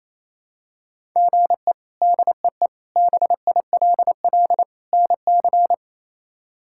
Morse code, heard as a keyed tone, about 28 wpm, 700 Hz. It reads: GE DEE BILL NC
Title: K1USN SST QSO Practice - 28wpm